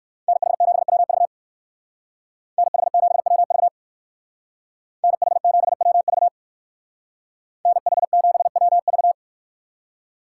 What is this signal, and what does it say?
Morse code, heard as a keyed tone, about 58 wpm, 700 Hz. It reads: NH7WV NH7WV NH7WV NH7WV